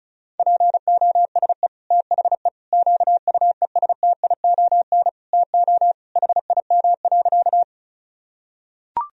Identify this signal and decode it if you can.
Morse code, heard as a keyed tone, about 34 wpm, 700 Hz. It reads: POSE THE QUESTION TO HIM.
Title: Sentences from Top 1000 Words - Review Part 2 - 35wpm